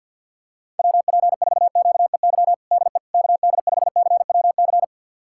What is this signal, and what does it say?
Morse code, heard as a keyed tone, about 47 wpm, 700 Hz. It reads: WP4QEY DE KD5CWC
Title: USA Callsign DE Callsign 50wpm